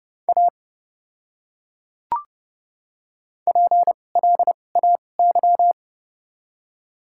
Morse code, heard as a keyed tone, about 30 wpm, spaced as for 22 wpm, 700 Hz. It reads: A PLAY
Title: 2x Character Spacing - Top 200 Words - 30wpm